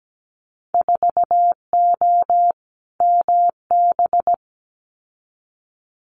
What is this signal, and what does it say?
Morse code, heard as a keyed tone, about 17 wpm, 700 Hz. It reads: 4O MB
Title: Winter Field Day Exchanges Encore 17wpm